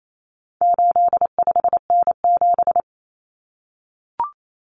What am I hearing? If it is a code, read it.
Morse code, heard as a keyed tone, about 28 wpm, 700 Hz. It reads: 85N7
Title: Common Cut Numbers - 4-Digit 28wpm